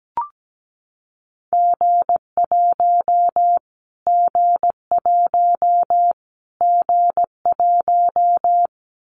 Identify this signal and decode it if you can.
Morse code, heard as a keyed tone, about 17 wpm, 700 Hz. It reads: G1 G1 G1
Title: Callsign Prefixes for Top 25 DXCC - Repeated 3x - 17wpm